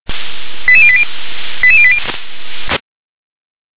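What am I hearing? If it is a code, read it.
Five-tone selective call (ZVEI), heard as quick runs of five tones, tones of about 70 ms. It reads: 88080 88080
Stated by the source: ZVEI five-tone selective call, received by SDR.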